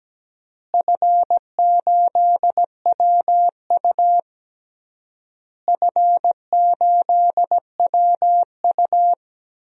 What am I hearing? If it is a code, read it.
Morse code, heard as a keyed tone, about 17 wpm, 700 Hz. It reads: F8WU F8WU
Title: DXCC-100 Call Signs - 4 Characters - Repeated 3x - 17wpm